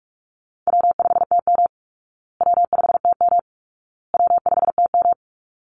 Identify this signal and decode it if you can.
Morse code, heard as a keyed tone, about 44 wpm, 700 Hz. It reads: W5TM W5TM W5TM
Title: Call Signs Repeated 2x - 45wpm